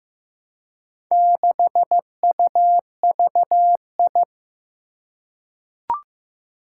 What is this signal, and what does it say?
Morse code, heard as a keyed tone, about 15 wpm, 700 Hz. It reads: 6UVI
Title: Four Challenging Letter-Number Combinations 15wpm